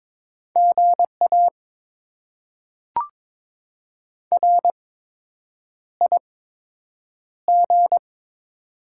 Morse code, heard as a keyed tone, about 22 wpm, spaced as for 8 wpm, 700 Hz. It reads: GA RIG
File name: Top 20 QSO Elements - 8x Character Spacing - 22wpm